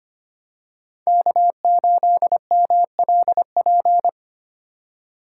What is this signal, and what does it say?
Morse code, heard as a keyed tone, about 25 wpm, 700 Hz. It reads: K8MLP
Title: USA Call Signs - 5 Characters - 4x Character Spacing - 25wpm